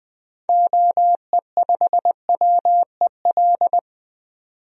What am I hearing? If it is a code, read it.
Morse code, heard as a keyed tone, about 21 wpm, 700 Hz. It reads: OE5WEL
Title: DXCC-100 Call Signs - 6 Characters - 20wpm